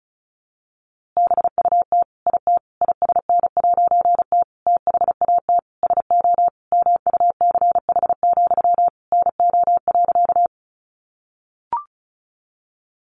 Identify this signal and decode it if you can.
Morse code, heard as a keyed tone, about 34 wpm, 700 Hz. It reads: BUT IT ISN'T THAT SO MUCH, NO.
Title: A Sound Of Thunder 35wpm